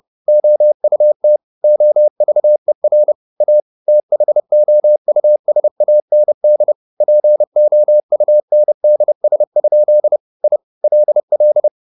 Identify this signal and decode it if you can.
Morse code, heard as a keyed tone, about 30 wpm, 600 Hz. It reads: OUT OVER A THOUSAND POUNDS? I LL
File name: the old man and the sea / the old man and the sea 1 30wpm